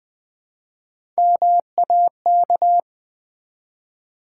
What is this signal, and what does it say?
Morse code, heard as a keyed tone, about 20 wpm, 700 Hz. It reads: MAK
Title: Build A Word - 1 Letter At A Time - 4-Letter Words - 20wpm